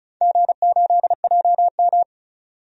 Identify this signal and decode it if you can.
Morse code, heard as a keyed tone, about 36 wpm, 700 Hz. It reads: G8JM
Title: DXCC-100 Call Signs - 4 Characters - 35wpm